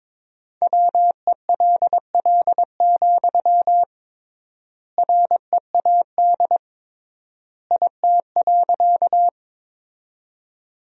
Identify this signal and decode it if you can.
Morse code, heard as a keyed tone, about 22 wpm, 700 Hz. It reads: WELL, READ IT.
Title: 3x Word-Spacing - Sentences from Top 200 Words - 22wpm